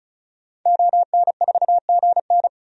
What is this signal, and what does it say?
Morse code, heard as a keyed tone, about 34 wpm, 700 Hz. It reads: ON4GN
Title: DXCC-100 Call Signs - 5 Characters - 35wpm